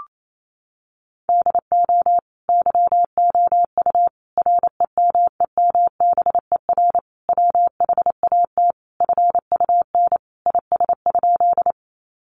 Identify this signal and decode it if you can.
Morse code, heard as a keyed tone, about 28 wpm, 700 Hz. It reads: DO YOU REMEMBER WHAT FUN IS?
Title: Sentences from Top 700 Words 28wpm